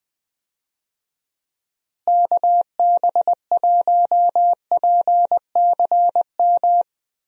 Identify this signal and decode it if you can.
Morse code, heard as a keyed tone, about 21 wpm, 700 Hz. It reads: KB1PCM
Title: USA Call Signs - 6 Characters - Repeated 3x - 20wpm